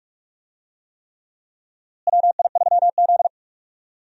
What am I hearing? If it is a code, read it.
Morse code, heard as a keyed tone, about 46 wpm, 700 Hz. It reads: WI3Z
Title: USA Call Signs - 4 Characters - Repeated 3x - 45wpm